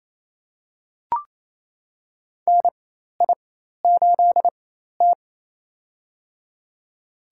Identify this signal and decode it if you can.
Morse code, heard as a keyed tone, about 27 wpm, spaced as for 13 wpm, 700 Hz. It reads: NI8T
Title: USA Call Signs - 4 Characters - 4x Character Spacing - 28wpm